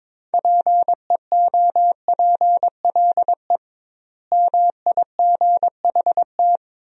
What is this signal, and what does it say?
Morse code, heard as a keyed tone, about 22 wpm, 700 Hz. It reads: PEOPLE MIGHT
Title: 2x Word-Spacing - Sentences from Top 500 Words - 22wpm